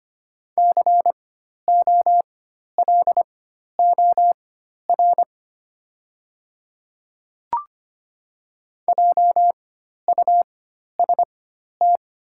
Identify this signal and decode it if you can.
Morse code, heard as a keyed tone, about 25 wpm, spaced as for 12 wpm, 700 Hz. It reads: COLOR JUST
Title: 4x Character Spacing - Top 300 Words - 25wpm